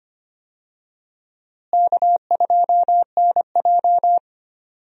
Morse code, heard as a keyed tone, about 25 wpm, 700 Hz. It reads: K2NJ
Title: USA Call Signs - 4 Characters - Repeated 2x - 25wpm